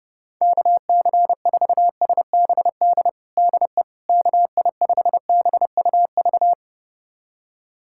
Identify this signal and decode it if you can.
Morse code, heard as a keyed tone, about 30 wpm, 700 Hz. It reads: KC4SBD DE KI5BUV
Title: USA Callsign DE Callsign - 6 Characters - 30wpm